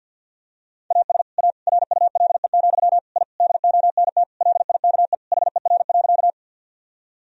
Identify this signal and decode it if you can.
Morse code, heard as a keyed tone, about 51 wpm, 700 Hz. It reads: AS A RULE, I DONT LIKE HER.
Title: Sentences from Top 1000 Words - Review Part 2 - 50wpm